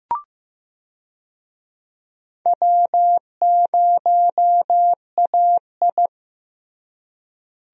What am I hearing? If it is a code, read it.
Morse code, heard as a keyed tone, about 15 wpm, 700 Hz. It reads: W0AI
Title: USA Call Signs - 4 Characters - Repeated 2x - 15wpm